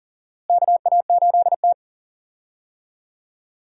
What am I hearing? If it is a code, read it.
Morse code, heard as a keyed tone, about 40 wpm, 700 Hz. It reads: KA8T
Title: USA Call Signs - 4 Characters - Repeated 3x - 40wpm